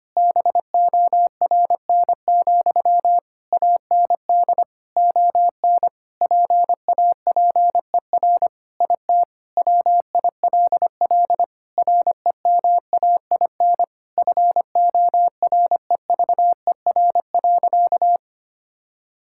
Morse code, heard as a keyed tone, about 25 wpm, 700 Hz. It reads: BORN, AND ON PAPER IT WILL REMAIN FOREVER.
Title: Sentences from Top 1600 Words 25wpm